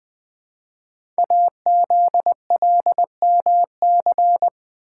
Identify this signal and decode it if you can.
Morse code, heard as a keyed tone, about 20 wpm, 700 Hz. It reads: AZLMC
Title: Code Groups - 5 Letters - 20wpm